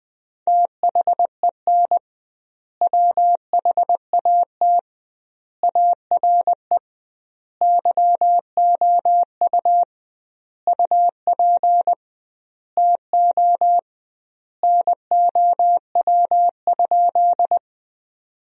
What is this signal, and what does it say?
Morse code, heard as a keyed tone, about 21 wpm, 700 Hz. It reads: THEN WHAT ARE YOU UP TO NOW?
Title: 2x Word-Spacing - Sentences from Top 100 Words - 20wpm